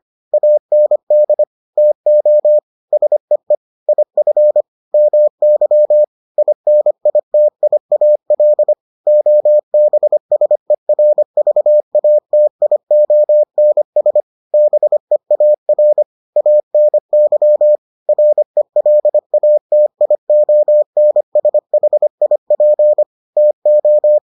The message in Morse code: AND TO SEE IF MY INITIAL OBSERVATIONS BEAR ANY RELATIONSHIP TO

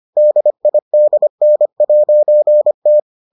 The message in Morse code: DIDN'T

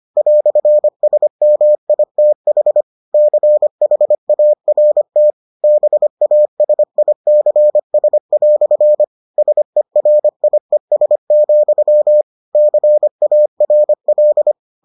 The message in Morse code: "SMITH CHART BASICS" SERIES, CARL

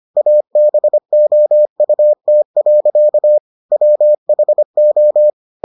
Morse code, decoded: ABOUT. WHO

